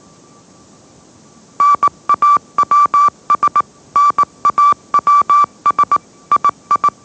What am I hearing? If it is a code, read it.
Morse code, heard as a keyed tone, about 18 wpm, 1160 Hz. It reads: NAWS NAWS II
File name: mrx / mrx082 20wpm